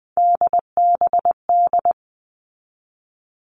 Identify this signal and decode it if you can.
Morse code, heard as a keyed tone, about 20 wpm, 700 Hz. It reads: DBD
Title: All Cut Numbers - 3-Digit 20wpm